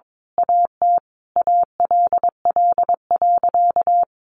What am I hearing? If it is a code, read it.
Morse code, heard as a keyed tone, about 22 wpm, 700 Hz. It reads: AT ALL.
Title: Sentences from Top 1300 Words 22wpm